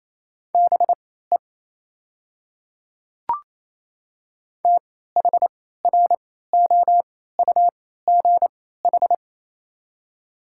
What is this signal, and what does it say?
Morse code, heard as a keyed tone, about 28 wpm, spaced as for 16 wpm, 700 Hz. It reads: BE THROUGH
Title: 3x Character Spacing - Top 200 Words - 28wpm